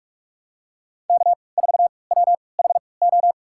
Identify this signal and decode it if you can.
Morse code, heard as a keyed tone, about 46 wpm, spaced as for 26 wpm, 700 Hz. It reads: K4WHO
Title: USA Call Signs - 5 Characters - 3x Character Spacing - 45wpm